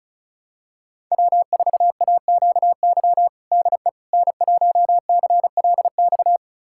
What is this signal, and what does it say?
Morse code, heard as a keyed tone, about 34 wpm, 700 Hz. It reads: W4AQY DE N1CLX
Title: USA Callsign DE Callsign - 5 Characters - 35wpm